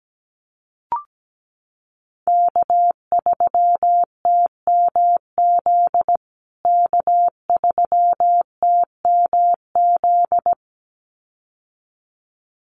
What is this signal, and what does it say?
Morse code, heard as a keyed tone, about 17 wpm, 700 Hz. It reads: K3TMZ K3TMZ
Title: USA Call Signs Repeated 2x 17wpm